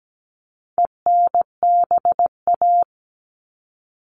Morse code, heard as a keyed tone, about 17 wpm, 700 Hz. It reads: ENBA
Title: All Cut Numbers - 4-Digit 17wpm